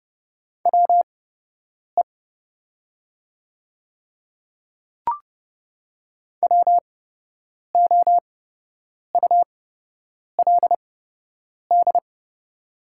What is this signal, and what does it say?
Morse code, heard as a keyed tone, about 29 wpm, spaced as for 8 wpm, 700 Hz. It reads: WE WOULD
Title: 8x Character Spacing - Top 100 Words - 30wpm